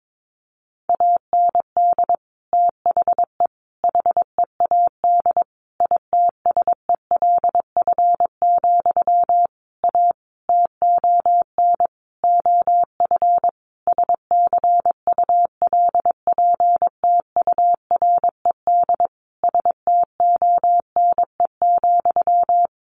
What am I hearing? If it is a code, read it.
Morse code, heard as a keyed tone, about 22 wpm, 700 Hz. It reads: AND THE HEAD ITSELF, A TON OF SCULPTURED STONE,
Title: A Sound Of Thunder 22wpm